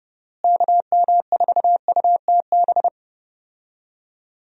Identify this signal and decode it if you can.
Morse code, heard as a keyed tone, about 30 wpm, 700 Hz. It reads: KM4UTB
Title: USA Call Signs - 6 Characters - Repeated 2x - 30wpm